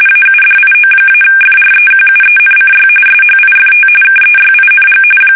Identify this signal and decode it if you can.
RTTY, about 45.45 baud. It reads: AUDIO-FREQUENCY SHIFT KEYING
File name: fsk850 / teletype-RTTY